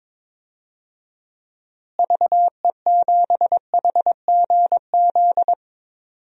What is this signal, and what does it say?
Morse code, heard as a keyed tone, about 22 wpm, 700 Hz. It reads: VE7HGZ
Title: DXCC-100 Call Signs - 6 Characters - Repeated 2x - 22wpm